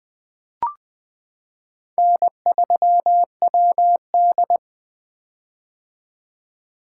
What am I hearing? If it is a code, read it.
Morse code, heard as a keyed tone, about 20 wpm, 700 Hz. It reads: N3WD
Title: 2023 POTA Activator Call Signs - 20wpm